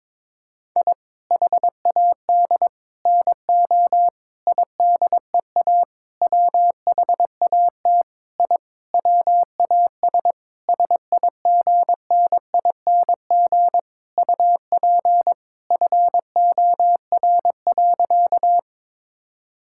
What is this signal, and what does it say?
Morse code, heard as a keyed tone, about 22 wpm, 700 Hz. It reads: I HAD NO IDEA WHAT I WAS SIGNING UP FOR.
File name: Sentences with 7-Word Phrases - 22wpm